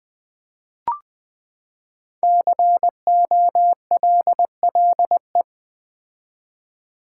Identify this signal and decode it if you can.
Morse code, heard as a keyed tone, about 20 wpm, 700 Hz. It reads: COLLE
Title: Build A Word - 5 Letters At A Time - 10-Letter Words - 20wpm